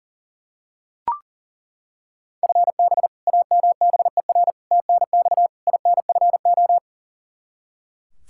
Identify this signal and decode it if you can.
Morse code, heard as a keyed tone, about 37 wpm, 700 Hz. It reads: FB AMBER TNX INFO
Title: Practice QSO - Exchange 3 - More Info - 40wpm